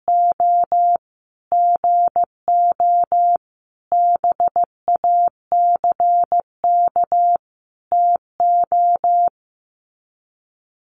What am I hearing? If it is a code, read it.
Morse code, heard as a keyed tone, about 15 wpm, 700 Hz. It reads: O GO BACK TO
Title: Sets of 5 Words Encore2 15wpm